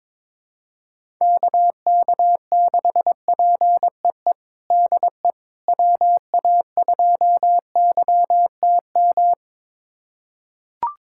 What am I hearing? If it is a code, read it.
Morse code, heard as a keyed tone, about 22 wpm, 700 Hz. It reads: KK6PEE DE WA2YTM
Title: USA Callsign DE Callsign - 6 Characters - 22wpm